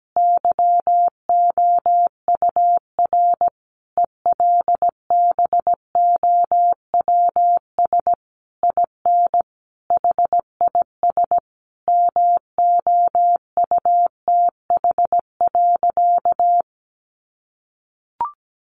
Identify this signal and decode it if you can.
Morse code, heard as a keyed tone, about 17 wpm, 700 Hz. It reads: YOUR ELBOWS IN HIS MOUTH.
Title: A Sound Of Thunder 17wpm